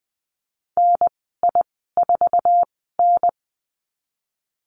Morse code, heard as a keyed tone, about 20 wpm, 700 Hz. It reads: N I 4 N
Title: 055 Lesson 10g - Letter r - Call Signs 20wpm